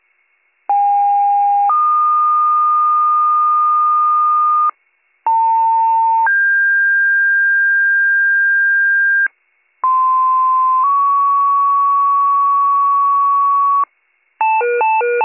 Two-tone page, heard as a one-second tone, then a three-second tone, about 810 then 1220 Hz, about 900 then 1600 Hz, about 1040 then 1120 Hz. It is heard from 0.7 to 13.8 s.